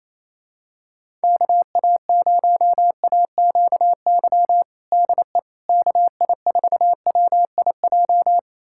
Morse code, heard as a keyed tone, about 28 wpm, 700 Hz. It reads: KA0AQY DE KI4WIJ
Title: USA Callsign DE Callsign - 6 Characters - 28wpm